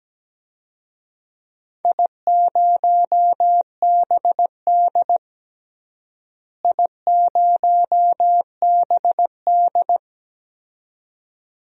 Morse code, heard as a keyed tone, about 17 wpm, 700 Hz. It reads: I0BD I0BD
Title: DXCC-100 Call Signs - 4 Characters - Repeated 2x - 17wpm